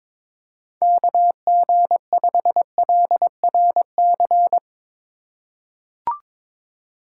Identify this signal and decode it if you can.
Morse code, heard as a keyed tone, about 22 wpm, 700 Hz. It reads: KG5LRC
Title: USA Call Signs - 6 Characters - 2x Character Spacing - 22wpm